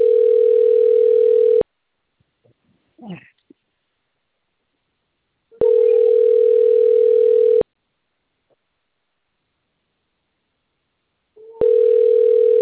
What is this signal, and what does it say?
Ringback tone (North America), 440 + 480 Hz.
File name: out- - - - - .7175